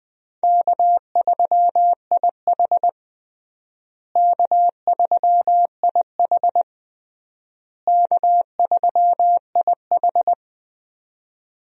Morse code, heard as a keyed tone, about 20 wpm, 700 Hz. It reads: K3IH K3IH K3IH
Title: USA Call Signs - 4 Characters - Repeated 3x - 20wpm